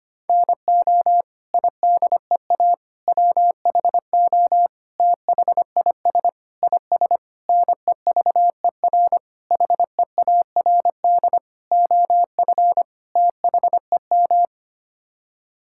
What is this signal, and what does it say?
Morse code, heard as a keyed tone, about 25 wpm, 700 Hz. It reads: NO IDEA WHO THIS IS NEVER HEARD OF THEM
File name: Sets of 9 Words - 25wpm